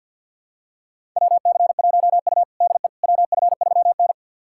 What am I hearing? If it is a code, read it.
Morse code, heard as a keyed tone, about 52 wpm, 700 Hz. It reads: WC1U DE WF3N
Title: USA Callsign DE Callsign - 4 Characters - 50wpm